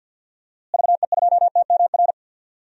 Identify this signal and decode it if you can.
Morse code, heard as a keyed tone, about 52 wpm, 700 Hz. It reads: VE2TMR
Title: DXCC-100 Call Signs - 6 Characters - Repeated 3x - 50wpm